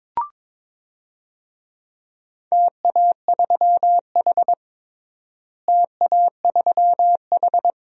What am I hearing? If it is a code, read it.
Morse code, heard as a keyed tone, about 22 wpm, 700 Hz. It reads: TA3H TA3H
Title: DXCC-100 Call Signs - 4 Characters - Repeated 2x - 22wpm